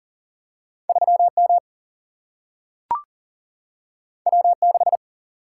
Morse code, heard as a keyed tone, about 40 wpm, 700 Hz. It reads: 3 M W 6
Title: Single-Letter Single-Number 40wpm